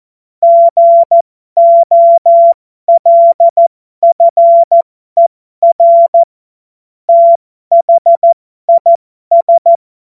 Morse code, heard as a keyed tone, about 14 wpm, spaced as for 12 wpm, 680 Hz. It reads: GOLFER THIS
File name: Morsekurs / Witze